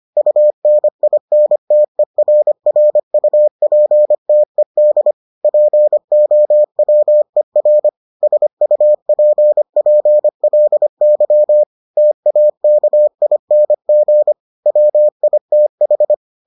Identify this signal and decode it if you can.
Morse code, heard as a keyed tone, about 25 wpm, 600 Hz. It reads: UNINTERRUPTED POWER SUPPLY TAKING WITH